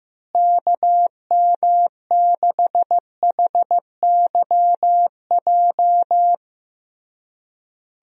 Morse code, heard as a keyed tone, about 16 wpm, 700 Hz. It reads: KM6HYJ
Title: USA Call Signs - 6 Characters - Repeated 3x - 15wpm